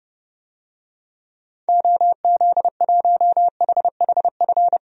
Morse code, heard as a keyed tone, about 30 wpm, 700 Hz. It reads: OZ1HHF
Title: DXCC-100 Call Signs - 6 Characters - 30wpm